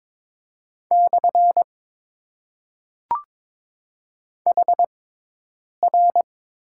Morse code, heard as a keyed tone, about 22 wpm, spaced as for 8 wpm, 700 Hz. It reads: / HR
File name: Top 90 QSO Elements - 6x Character Spacing - 22wpm